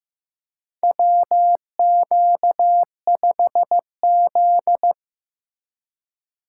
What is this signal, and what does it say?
Morse code, heard as a keyed tone, about 15 wpm, 700 Hz. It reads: WQ5Z
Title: USA Call Signs - 4 Characters - 3x Character Spacing - 15wpm